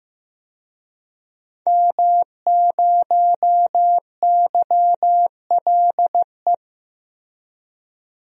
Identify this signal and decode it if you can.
Morse code, heard as a keyed tone, about 15 wpm, 700 Hz. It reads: M0YLE
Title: DXCC-100 Call Signs - 5 Characters - Repeated 2x - 15wpm